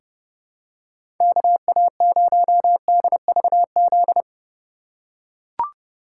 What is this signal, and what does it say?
Morse code, heard as a keyed tone, about 30 wpm, 700 Hz. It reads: KA0DVZ
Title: USA Call Signs - 6 Characters - 2x Character Spacing - 30wpm